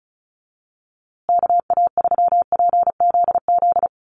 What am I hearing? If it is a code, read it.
Morse code, heard as a keyed tone, about 34 wpm, 700 Hz. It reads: KA3PZZ
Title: USA Call Signs Repeated 2x 35wpm